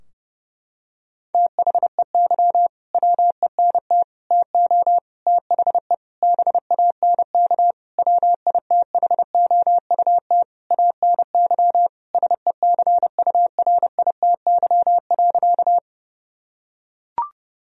Morse code, heard as a keyed tone, about 30 wpm, 700 Hz. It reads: THEY WENT TO THE BANK WITHOUT ANY SECURITY.
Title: Sentences from Top 1000 Words - Review Part 4 - 30wpm